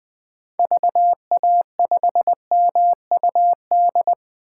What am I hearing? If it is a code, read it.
Morse code, heard as a keyed tone, about 21 wpm, 700 Hz. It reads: VA5MUD
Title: 2023 POTA Activator Call Signs - 20wpm